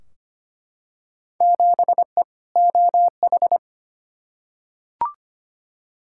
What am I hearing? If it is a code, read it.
Morse code, heard as a keyed tone, about 25 wpm, 700 Hz. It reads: 7E OH
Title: Field Day Exchanges Encore - 25wpm